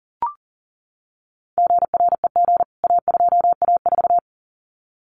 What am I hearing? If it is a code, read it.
Morse code, heard as a keyed tone, about 40 wpm, 700 Hz. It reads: GREG A2A4
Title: CWT Exchange 40wpm